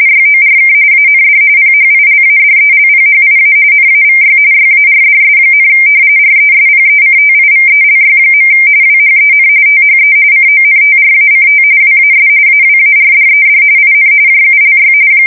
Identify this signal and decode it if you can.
RTTY, about 45.45 baud. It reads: YORK AND PHILADELPHIA NAVY YARDS, RESPECTIVELY. WHEN THE CONTRACT WAS AWARDED IN JULY
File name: fsk170 / battleshipnj